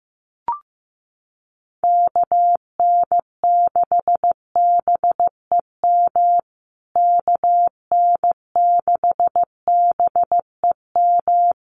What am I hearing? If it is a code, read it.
Morse code, heard as a keyed tone, about 15 wpm, 700 Hz. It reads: KN6BEM KN6BEM
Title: USA Call Signs Repeated 3x 15wpm